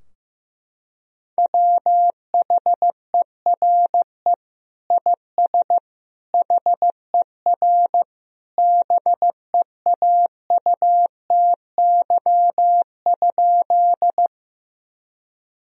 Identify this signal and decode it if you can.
Morse code, heard as a keyed tone, about 15 wpm, 700 Hz. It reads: WHERE IS HER BEAUTY?
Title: Sentences from Top 1000 Words - Review Part 2 - 15wpm